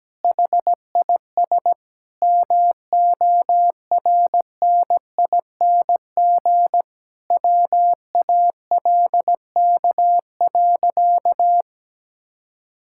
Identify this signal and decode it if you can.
Morse code, heard as a keyed tone, about 17 wpm, 700 Hz. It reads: HIS MORNING WALK.
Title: Sentences from Top 1000 Words - Review Part 5 - 17wpm